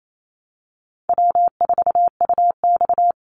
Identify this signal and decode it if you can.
Morse code, heard as a keyed tone, about 28 wpm, 700 Hz. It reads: W4UX
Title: Call Signs - 28wpm